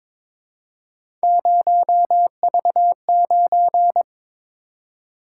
Morse code, heard as a keyed tone, about 22 wpm, 700 Hz. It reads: 0V9
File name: practice / 22wpm / Three Letter-Number Combinations - 22wpm